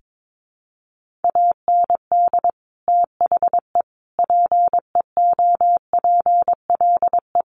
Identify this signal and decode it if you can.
Morse code, heard as a keyed tone, about 22 wpm, 700 Hz. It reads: AND THE PEOPLE
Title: Sentences from Top 1500 Words 22wpm